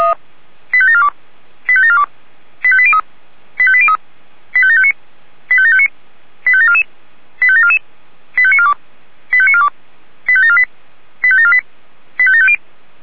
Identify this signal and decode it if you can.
Five-tone selective call (ZVEI), heard as quick runs of five tones, tones of about 70 ms. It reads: 85742 85742 85803 85803 85759 85759 85744 85744 85942 85942 85748 85748 85750
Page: SDR/ZVEI